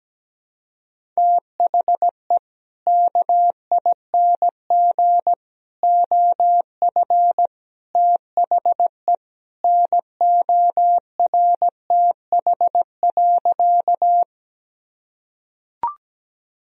Morse code, heard as a keyed tone, about 17 wpm, 700 Hz. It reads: THE KING OF THE NORTH.
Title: Sentences from Top 1000 Words - Review Part 1 - 17wpm